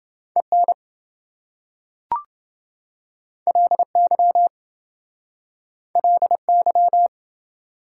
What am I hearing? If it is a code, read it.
Morse code, heard as a keyed tone, about 26 wpm, 700 Hz. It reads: EN LY LY